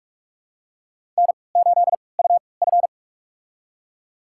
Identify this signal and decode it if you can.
Morse code, heard as a keyed tone, about 46 wpm, spaced as for 26 wpm, 700 Hz. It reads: N8UF
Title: USA Call Signs - 4 Characters - 3x Character Spacing - 45wpm